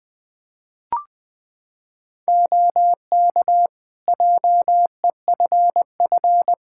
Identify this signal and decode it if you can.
Morse code, heard as a keyed tone, about 20 wpm, 700 Hz. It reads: OK JEFF
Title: Practice QSO - Exchange 4 - Ending - 20wpm